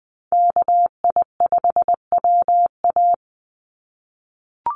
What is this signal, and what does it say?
Morse code, heard as a keyed tone, about 20 wpm, 700 Hz. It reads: KI5WA
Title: NA SOTA Chaser Callsigns 20wpm